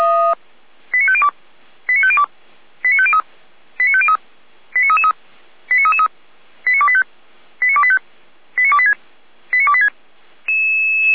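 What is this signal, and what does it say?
Five-tone selective call (ZVEI), heard as quick runs of five tones, tones of about 70 ms. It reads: 89502 89502 89503 89503 89333 89333 89285 89285 89286 89286
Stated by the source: ZVEI five-tone selective call 89097, received by SDR.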